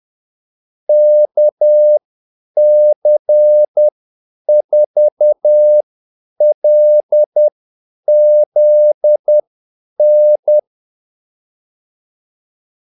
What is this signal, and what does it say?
Morse code, heard as a keyed tone, about 10 wpm, spaced as for 8 wpm, 600 Hz. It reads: KC4LZN
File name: Callsigns-10-8wpm